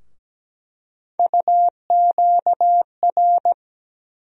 Morse code, heard as a keyed tone, about 17 wpm, 700 Hz. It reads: UQR
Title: USA Call Sign Postfixes - 3 Letters 17wpm